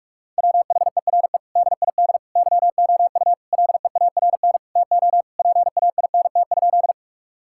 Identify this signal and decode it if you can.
Morse code, heard as a keyed tone, about 41 wpm, 700 Hz. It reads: WHERE DID YOU LEARN TO PAINT?
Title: Sentences from Top 1000 Words - Review Part 6 - 45wpm